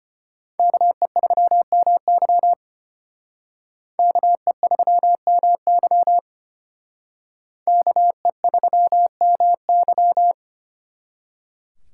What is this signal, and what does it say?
Morse code, heard as a keyed tone, about 31 wpm, 700 Hz. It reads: KE3MY KE3MY KE3MY
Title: Speed Racing - USA 2x2 Call Signs 25wpm